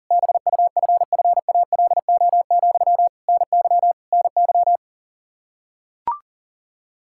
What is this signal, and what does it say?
Morse code, heard as a keyed tone, about 40 wpm, 700 Hz. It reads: BUFFALO, NY NY
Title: K1USN SST QSO Practice 40wpm